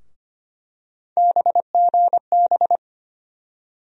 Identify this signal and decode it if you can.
Morse code, heard as a keyed tone, about 25 wpm, 700 Hz. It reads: BGB
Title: USA Call Sign Postfixes - 3 Letters 25wpm